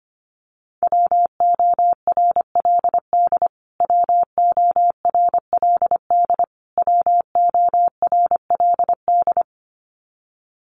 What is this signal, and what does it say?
Morse code, heard as a keyed tone, about 25 wpm, 700 Hz. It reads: WORLD WORLD WORLD
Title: Top 500 Words Repeated 25wpm